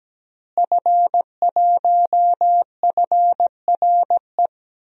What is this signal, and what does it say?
Morse code, heard as a keyed tone, about 17 wpm, 700 Hz. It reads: F1FRE
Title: DXCC-100 Call Signs - 5 Characters - Repeated 3x - 17wpm